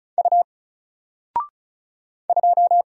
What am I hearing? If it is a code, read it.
Morse code, heard as a keyed tone, about 34 wpm, 700 Hz. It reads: U 2
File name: Single-Letter-Number-35wpm